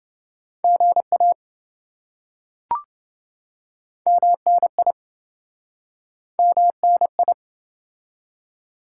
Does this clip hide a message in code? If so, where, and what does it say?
Yes: Morse code from 0.6 to 7.3 s, heard as a keyed tone, about 27 wpm, 700 Hz. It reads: GA MNI MNI